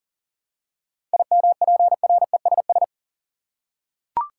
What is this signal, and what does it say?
Morse code, heard as a keyed tone, about 40 wpm, 700 Hz. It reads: IMPRESS
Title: Seven Letter Words - 40wpm